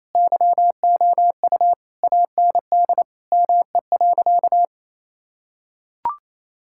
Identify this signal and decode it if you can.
Morse code, heard as a keyed tone, about 28 wpm, 700 Hz. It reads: YOU AND ME.
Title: Sentences from Top 1600 Words 28wpm